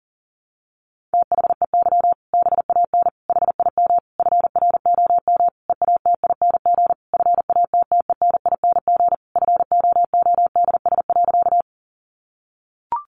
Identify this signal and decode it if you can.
Morse code, heard as a keyed tone, about 40 wpm, 700 Hz. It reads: THEY BAN HIM FROM EATING FATTENING FOODS.
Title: Easy Phrases 40wpm